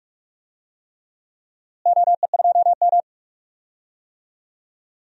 Morse code, heard as a keyed tone, about 47 wpm, 700 Hz. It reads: OE2M
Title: DXCC-100 Call Signs - 4 Characters - Repeated 3x - 45wpm